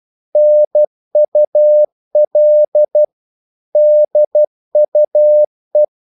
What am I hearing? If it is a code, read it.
Morse code, heard as a keyed tone, about 12 wpm, 600 Hz. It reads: NUL DUE
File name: Korte ord DK 12wpm